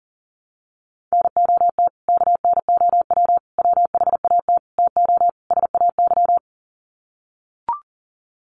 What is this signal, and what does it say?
Morse code, heard as a keyed tone, about 40 wpm, 700 Hz. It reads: NOT KNOW WHAT TO SAY
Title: Sets of 5 Words 40wpm